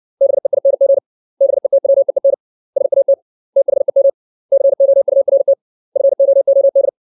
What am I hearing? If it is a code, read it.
Morse code, heard as a keyed tone, about 60 wpm, 550 Hz. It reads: BEING BETWEEN SAT THEM COURT FOOD
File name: 632 words / 632 Words 60WPM